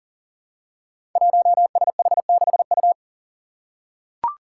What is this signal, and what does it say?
Morse code, heard as a keyed tone, about 40 wpm, 700 Hz. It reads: 1SH6U
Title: Five Challenging Letter-Number Combinations 40wpm